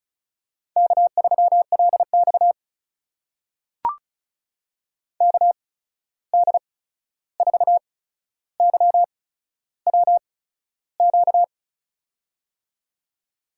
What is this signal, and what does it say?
Morse code, heard as a keyed tone, about 34 wpm, spaced as for 13 wpm, 700 Hz. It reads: K3LX KD4YWQ
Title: USA Call Signs - 8x Character Spacing - 35wpm